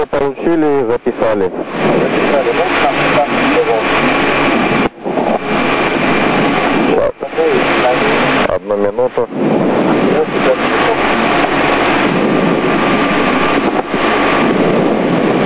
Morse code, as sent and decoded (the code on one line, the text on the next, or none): none